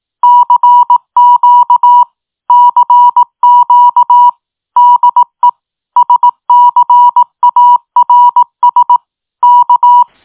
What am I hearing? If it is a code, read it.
Morse code, heard as a keyed tone, about 18 wpm, 970 Hz. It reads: CQ CQ DE SCARS K
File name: tin-upgrade / QSO 2024 01 18 .mp3